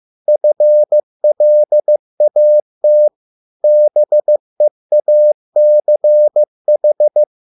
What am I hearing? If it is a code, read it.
Morse code, heard as a keyed tone, about 15 wpm, 600 Hz. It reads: FLAT BEACH